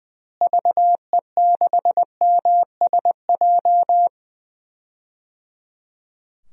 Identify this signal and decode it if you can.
Morse code, heard as a keyed tone, about 20 wpm, 700 Hz. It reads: VE6MSJ
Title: DXCC-100 Call Signs - 6 Characters - Repeated 3x - 20wpm